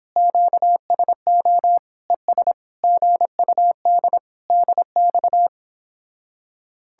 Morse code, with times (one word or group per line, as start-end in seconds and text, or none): 0.2-1.8 QSO
2.1-2.5 ES
2.8-4.2 GUD
4.5-5.5 DX